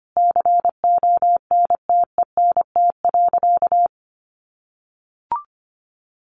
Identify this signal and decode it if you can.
Morse code, heard as a keyed tone, about 25 wpm, 700 Hz. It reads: CONTENT.
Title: Sentences from Top 900 Words 25wpm